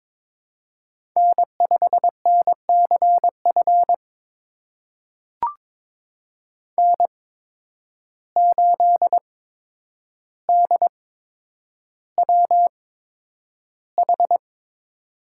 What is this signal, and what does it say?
Morse code, heard as a keyed tone, about 22 wpm, spaced as for 9 wpm, 700 Hz. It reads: N5NCF N8DWH
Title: USA Call Signs - 5 Characters - 8x Character Spacing - 22wpm